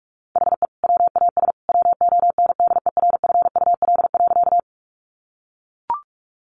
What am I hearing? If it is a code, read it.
Morse code, heard as a keyed tone, about 44 wpm, 700 Hz. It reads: HE WAS WONDERFUL.
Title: Sentences from Top 1100 Words 45wpm